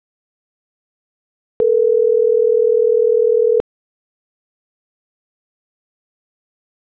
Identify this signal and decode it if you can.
Ringback tone (North America), 440 + 480 Hz.